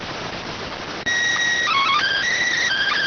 SSTV (slow-scan television), PD 290.